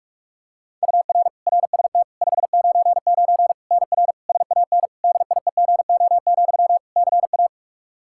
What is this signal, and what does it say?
Morse code, heard as a keyed tone, about 46 wpm, 700 Hz. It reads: UR RST 599 NR SAN DIEGO, CA
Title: Practice QSO - Exchange 1 - Beginning - 45wpm